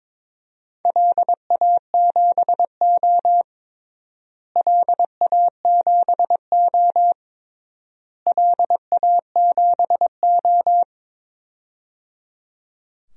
Morse code, heard as a keyed tone, about 22 wpm, 700 Hz. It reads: LA7O LA7O LA7O
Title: DXCC-100 Call Signs - 4 Characters - Repeated 3x - 22wpm